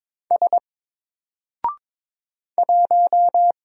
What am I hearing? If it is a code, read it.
Morse code, heard as a keyed tone, about 22 wpm, 700 Hz. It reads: S 1
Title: Single Character (Letter:Number:Punctuation:Prosign) Alt 22wpm